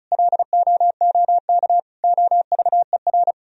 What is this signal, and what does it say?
Morse code, heard as a keyed tone, about 36 wpm, 700 Hz. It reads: LOOK OVER
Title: Sentences from Top 1000 Words - Review Part 2 - 35wpm